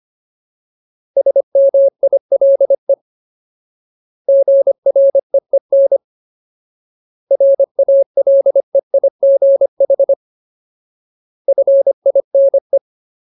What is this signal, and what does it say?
Morse code, heard as a keyed tone, about 25 wpm, 550 Hz. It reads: SMILE GREEN RALEIGH FINE
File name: Ultimate Word List / 25WPM Ultimate